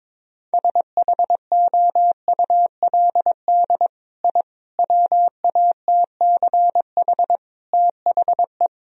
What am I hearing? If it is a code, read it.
Morse code, heard as a keyed tone, about 22 wpm, 700 Hz. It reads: SHOULD I WATCH THE
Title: Sentences from Top 1000 Words - Review Part 4 - 22wpm